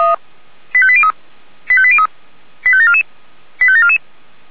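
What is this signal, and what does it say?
Five-tone selective call (ZVEI), heard as quick runs of five tones, tones of about 70 ms. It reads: 85803 85803 85744 85744
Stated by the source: ZVEI five-tone selective call received on an SDR.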